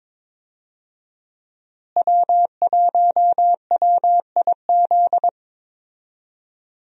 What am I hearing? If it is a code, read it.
Morse code, heard as a keyed tone, about 22 wpm, 700 Hz. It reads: W1WIZ
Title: USA Call Signs - 5 Characters - Repeated 3x - 22wpm